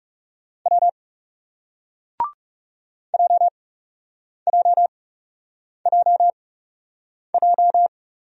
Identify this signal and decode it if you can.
Morse code, heard as a keyed tone, about 41 wpm, spaced as for 8 wpm, 700 Hz. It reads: W JJJJ